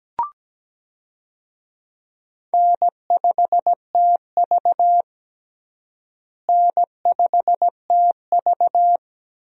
Morse code, heard as a keyed tone, about 17 wpm, 700 Hz. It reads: N5TV N5TV
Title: USA Call Signs - 4 Characters - Repeated 2x - 17wpm